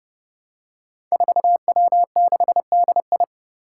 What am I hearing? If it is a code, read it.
Morse code, heard as a keyed tone, about 30 wpm, 700 Hz. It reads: 4W6DI
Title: Five Challenging Letter-Number Combinations 30wpm